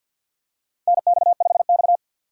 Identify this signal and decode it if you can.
Morse code, heard as a keyed tone, about 51 wpm, 700 Hz. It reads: NX5X
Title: USA Call Signs - 4 Characters - 50wpm